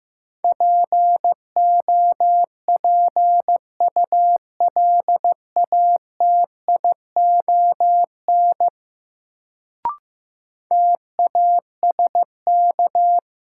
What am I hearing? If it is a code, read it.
Morse code, heard as a keyed tone, about 15 wpm, 700 Hz. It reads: POPULATION TASK
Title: Sentences from Top 1000 Words - Review Part 7 - New Word Drill - 15wpm